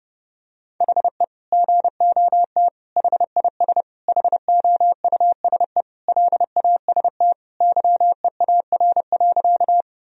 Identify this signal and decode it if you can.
Morse code, heard as a keyed tone, about 30 wpm, 700 Hz. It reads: HE GOT HIS HOUSE LAST YEAR.
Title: Sentences from Top 1000 Words - Review Part 1 - 30wpm